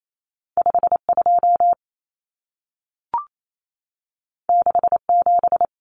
Morse code, heard as a keyed tone, about 28 wpm, 700 Hz. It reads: 52 67
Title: Common Cut Numbers - 2-Digit 28wpm